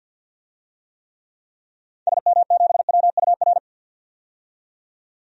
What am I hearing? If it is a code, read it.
Morse code, heard as a keyed tone, about 52 wpm, 700 Hz. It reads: SM7WUR
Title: DXCC-100 Call Signs - 6 Characters - 50wpm